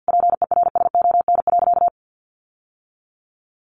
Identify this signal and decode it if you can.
Morse code, heard as a keyed tone, about 48 wpm, 700 Hz. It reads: PERSON.
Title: Sentences from Top 1300 Words 50wpm